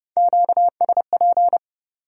Morse code, heard as a keyed tone, about 38 wpm, 700 Hz. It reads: QSP